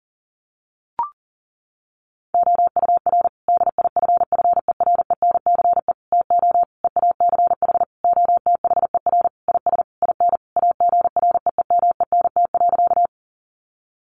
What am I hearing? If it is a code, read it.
Morse code, heard as a keyed tone, about 40 wpm, 700 Hz. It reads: OUR DIFFERENCE TO EACH OTHER IS IN AGREEMENT.
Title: Sentences from Top 1500 Words 40wpm